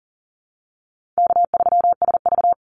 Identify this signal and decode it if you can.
Morse code, heard as a keyed tone, about 38 wpm, 700 Hz. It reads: K3SV
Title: Easy Call Signs - 40wpm